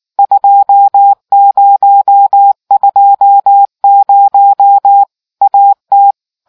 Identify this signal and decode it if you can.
Morse code, heard as a keyed tone, about 18 wpm, 800 Hz. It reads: 2020 AT